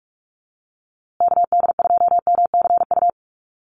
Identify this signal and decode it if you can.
Morse code, heard as a keyed tone, about 44 wpm, 700 Hz. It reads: KD2MCU
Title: USA Call Signs Repeated 2x 45wpm